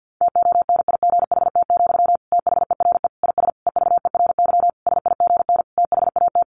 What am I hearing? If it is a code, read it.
Morse code, heard as a keyed tone, about 50 wpm, 700 Hz. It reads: TONIGHT, THERE IS EVERY SIGN THAT
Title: Sentences from Top 1400 Words 50wpm